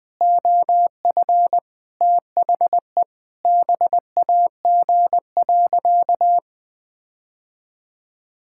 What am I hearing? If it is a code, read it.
Morse code, heard as a keyed tone, about 21 wpm, 700 Hz. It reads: OF THE BAG.
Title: Sentences with 7-Word Phrases - Spoken Beginning - 20wpm